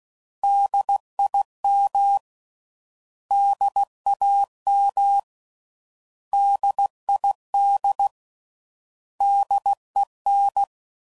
Morse code, heard as a keyed tone, about 16 wpm, 800 Hz. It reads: DIM DAM DID DEN
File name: Prefixes / By First Letter / Ds